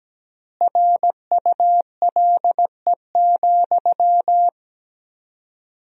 Morse code, heard as a keyed tone, about 17 wpm, 700 Hz. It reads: RULE,
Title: 4x Word-Spacing - Sentences from Top 400 Words - 17wpm